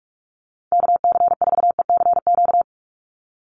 Morse code, heard as a keyed tone, about 44 wpm, 700 Hz. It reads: KC4ECQ
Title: USA Call Signs 45wpm